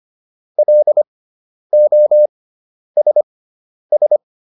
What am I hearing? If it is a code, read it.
Morse code, heard as a keyed tone, about 25 wpm, spaced as for 10 wpm, 600 Hz. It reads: LOSS